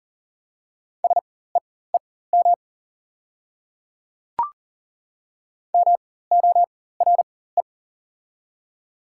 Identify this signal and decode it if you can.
Morse code, heard as a keyed tone, about 40 wpm, spaced as for 19 wpm, 700 Hz. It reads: SEEM MORE
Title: 4x Character Spacing - Top 300 Words - 40wpm